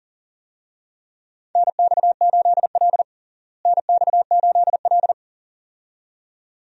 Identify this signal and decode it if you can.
Morse code, heard as a keyed tone, about 37 wpm, 700 Hz. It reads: NX8L NX8L
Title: USA Call Signs - 4 Characters - Repeated 2x - 40wpm